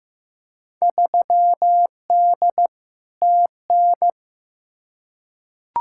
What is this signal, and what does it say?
Morse code, heard as a keyed tone, about 15 wpm, 700 Hz. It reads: 3D TN
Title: Field Day Exchanges Encore - 15wpm